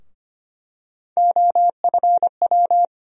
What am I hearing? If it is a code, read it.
Morse code, heard as a keyed tone, about 25 wpm, 700 Hz. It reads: OFW
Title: USA Call Sign Postfixes - 3 Letters 25wpm